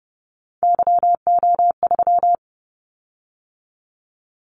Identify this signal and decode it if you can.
Morse code, heard as a keyed tone, about 30 wpm, 700 Hz. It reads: YO3
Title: Callsign Prefixes for Top 25 DXCC - 30wpm